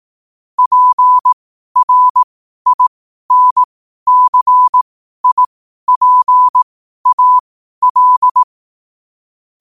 Morse code, heard as a keyed tone, about 18 wpm, spaced as for 12 wpm, 1000 Hz. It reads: PRINCIPAL
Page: 12 wpm morse code quotes